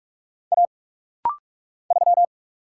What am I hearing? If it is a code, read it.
Morse code, heard as a keyed tone, about 44 wpm, 700 Hz. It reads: A 3
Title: Single Letter-Number - Rapid-Fire 45wpm